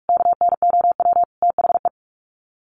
Morse code, heard as a keyed tone, about 44 wpm, 700 Hz. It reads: KNOW THE
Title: Sets of 5 Words 45wpm